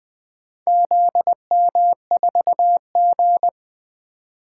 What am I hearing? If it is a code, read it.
Morse code, heard as a keyed tone, about 24 wpm, 700 Hz. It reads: ZM4G
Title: Speed-Racing - Four Letter-Number Combinations - 20wpm